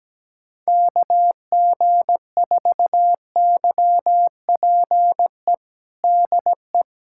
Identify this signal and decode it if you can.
Morse code, heard as a keyed tone, about 17 wpm, 700 Hz. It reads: KG4YPE DE
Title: USA Callsign DE Callsign 17wpm